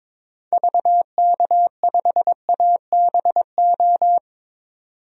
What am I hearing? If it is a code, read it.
Morse code, heard as a keyed tone, about 22 wpm, 700 Hz. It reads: VK5ABO
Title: DXCC-100 Call Signs - 6 Characters - 22wpm